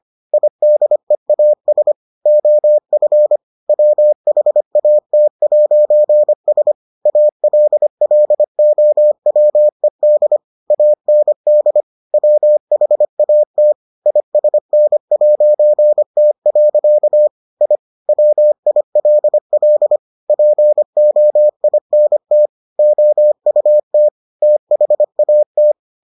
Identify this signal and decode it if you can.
Morse code, heard as a keyed tone, about 25 wpm, 600 Hz. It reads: IDEAS OF WHAT'S ALLOWED AND WHAT ISN'T. I WILL POINT OUT THAT